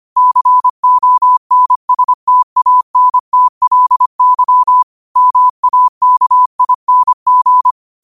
Morse code, heard as a keyed tone, about 25 wpm, 1000 Hz. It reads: CONSTANTLY MAKING
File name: qotdfiles / 25wpm